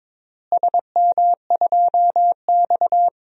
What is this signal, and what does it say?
Morse code, heard as a keyed tone, about 22 wpm, 700 Hz. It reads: SM2X
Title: DXCC-100 Call Signs - 4 Characters - Repeated 3x - 22wpm